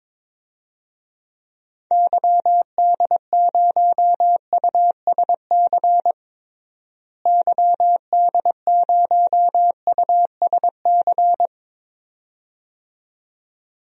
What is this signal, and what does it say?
Morse code, heard as a keyed tone, about 21 wpm, 700 Hz. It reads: YD0USC YD0USC
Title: DXCC-100 Call Signs - 6 Characters - Repeated 2x - 22wpm